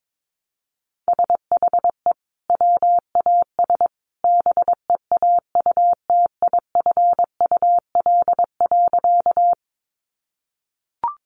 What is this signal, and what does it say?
Morse code, heard as a keyed tone, about 22 wpm, 700 Hz. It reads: SHE WAS BEAUTIFUL.
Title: Sentences from Top 900 Words 22wpm